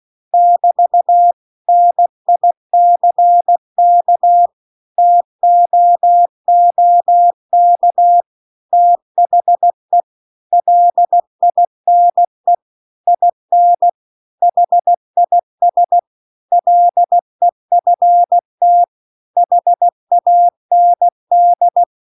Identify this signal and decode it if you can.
Morse code, heard as a keyed tone, about 16 wpm, 700 Hz. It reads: <BT> NICK TOOK THE LINE IN HIS LEFT HAND
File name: CW-The-Big-Twoheaded-River-2-16WPM